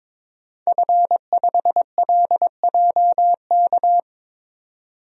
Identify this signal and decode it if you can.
Morse code, heard as a keyed tone, about 21 wpm, 700 Hz. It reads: F5LJK
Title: DXCC-100 Call Signs - 5 Characters - 22wpm